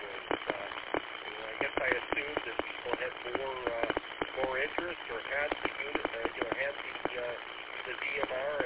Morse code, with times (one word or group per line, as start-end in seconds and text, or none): none